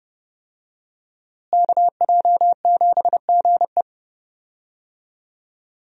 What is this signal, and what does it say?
Morse code, heard as a keyed tone, about 30 wpm, 700 Hz. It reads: KJ7GE
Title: USA Call Signs - 5 Characters - Repeated 3x - 30wpm